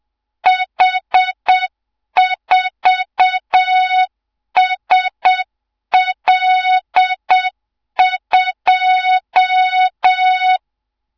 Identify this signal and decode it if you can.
Morse code, heard as a keyed tone, about 7 wpm, 3700 Hz, with noise. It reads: H4SL2
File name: Morze-HA2MN / hang / 24 GYAKORLAT